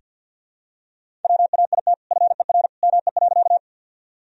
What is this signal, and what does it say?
Morse code, heard as a keyed tone, about 48 wpm, 700 Hz. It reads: WAIT FER ME.
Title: Sentences from Top 500 Words and Abbreviations 50wpm